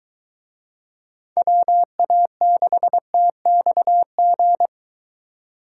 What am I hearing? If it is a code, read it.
Morse code, heard as a keyed tone, about 22 wpm, 700 Hz. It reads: WA6TXG